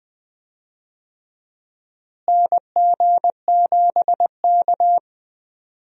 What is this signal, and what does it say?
Morse code, heard as a keyed tone, about 20 wpm, 700 Hz. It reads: NG7K
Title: USA Call Signs - 4 Characters - Repeated 2x - 20wpm